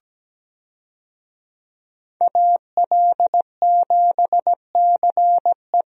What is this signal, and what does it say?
Morse code, heard as a keyed tone, about 17 wpm, 700 Hz. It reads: AL7CE
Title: USA Call Signs - 5 Characters - 17wpm